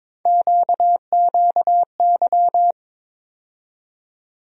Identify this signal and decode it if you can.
Morse code, heard as a keyed tone, about 22 wpm, 700 Hz. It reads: QQY
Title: USA Call Sign Postfixes - 3 Letters 22wpm